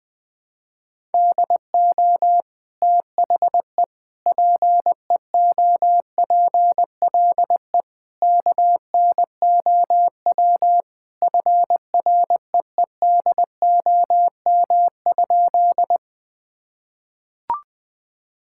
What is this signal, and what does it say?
Morse code, heard as a keyed tone, about 20 wpm, 700 Hz. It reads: DO THE PEOPLE KNOW FREEDOM?
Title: Sentences from Top 1000 Words - Review Part 5 - 20wpm